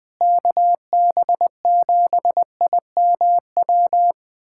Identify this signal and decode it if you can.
Morse code, heard as a keyed tone, about 20 wpm, 700 Hz. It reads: KB7IMW
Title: USA Callsign DE Callsign 20wpm